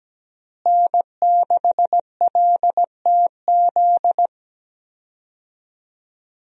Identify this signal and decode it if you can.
Morse code, heard as a keyed tone, about 17 wpm, 700 Hz. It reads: N6LTZ
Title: USA Call Signs - 5 Characters - Repeated 3x - 17wpm